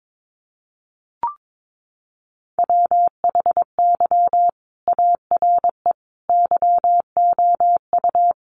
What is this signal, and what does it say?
Morse code, heard as a keyed tone, about 22 wpm, 700 Hz. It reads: WHY ARE YOU
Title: Sentences from Top 200 Words 22wpm